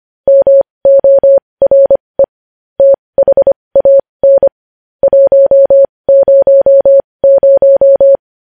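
Morse code, heard as a keyed tone, about 25 wpm, spaced as for 20 wpm, 550 Hz. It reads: MORE THAN 100
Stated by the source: Morse code news headlines sent at 20 wpm.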